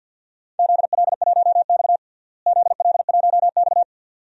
Morse code, heard as a keyed tone, about 51 wpm, 700 Hz. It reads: ZL1X ZL1X
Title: DXCC-100 Call Signs - 4 Characters - Repeated 2x - 50wpm